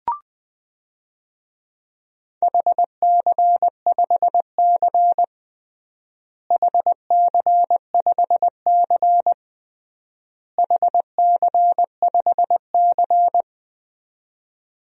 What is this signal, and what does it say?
Morse code, heard as a keyed tone, about 20 wpm, 700 Hz. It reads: HC5C HC5C HC5C
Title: DXCC-100 Call Signs - 4 Characters - Repeated 3x - 20wpm